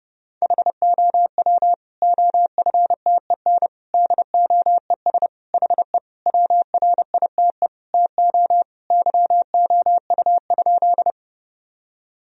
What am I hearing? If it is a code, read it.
Morse code, heard as a keyed tone, about 30 wpm, 700 Hz. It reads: HOW OFTEN DOES HE WRITE TO YOU?
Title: Sentences from Top 1000 Words - Review Part 1 - 30wpm